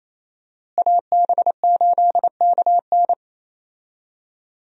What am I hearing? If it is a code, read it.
Morse code, heard as a keyed tone, about 27 wpm, 700 Hz. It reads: AB8KN
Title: USA Call Signs - 5 Characters - Repeated 2x - 28wpm